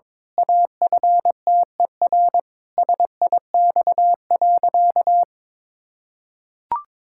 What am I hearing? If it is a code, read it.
Morse code, heard as a keyed tone, about 22 wpm, 700 Hz. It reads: AFTER SIX.
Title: Pangram Challenge v2 - 22wpm